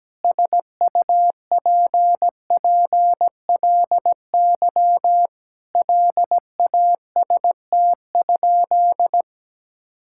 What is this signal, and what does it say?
Morse code, heard as a keyed tone, about 17 wpm, 700 Hz. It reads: SUPPLY LAST?
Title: Sentences from Top 1000 Words - Review Part 5 - 17wpm